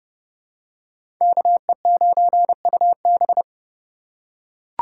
Morse code, heard as a keyed tone, about 29 wpm, 700 Hz. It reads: KE9UB
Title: USA Call Signs - 5 Characters - 4x Character Spacing - 30wpm